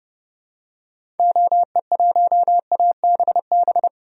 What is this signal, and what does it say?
Morse code, heard as a keyed tone, about 30 wpm, 700 Hz. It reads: OE1ABB
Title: DXCC-100 Call Signs - 6 Characters - 30wpm